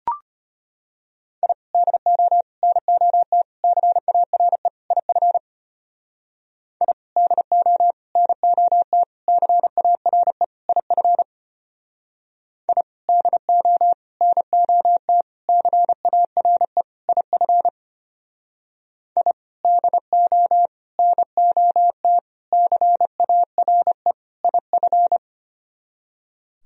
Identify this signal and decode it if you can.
Morse code, heard as a keyed tone, about 32 wpm, 700 Hz. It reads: I DO NOT CARE IF I DO NOT CARE IF I DO NOT CARE IF I DO NOT CARE IF